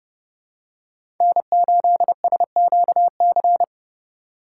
Morse code, heard as a keyed tone, about 30 wpm, 700 Hz. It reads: N8SQC
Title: USA Call Signs - 4x Character Spacing - 30wpm